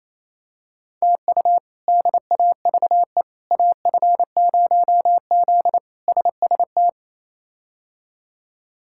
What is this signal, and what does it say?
Morse code, heard as a keyed tone, about 28 wpm, 700 Hz. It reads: TU DAVE AF0Z SST
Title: K1USN SST QSO Practice - 28wpm